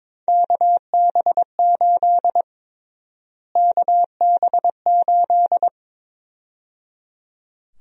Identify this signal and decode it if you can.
Morse code, heard as a keyed tone, about 22 wpm, 700 Hz. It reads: KB8 KB8
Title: USA Call Sign Prefixes - Repeated 2x - 22wpm